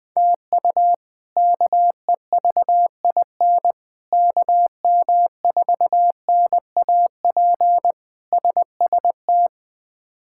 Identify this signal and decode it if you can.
Morse code, heard as a keyed tone, about 20 wpm, 700 Hz. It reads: TU KEVIN KM4NAP SST
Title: K1USN SST QSO Practice - 20wpm